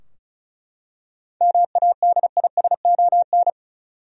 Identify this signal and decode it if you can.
Morse code, heard as a keyed tone, about 34 wpm, 700 Hz. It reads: MADISON
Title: USA Common Names - 35wpm